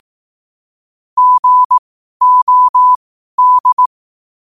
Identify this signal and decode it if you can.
Morse code, heard as a keyed tone, about 18 wpm, spaced as for 12 wpm, 1000 Hz. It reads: GOD
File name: qotdfiles / 12wpm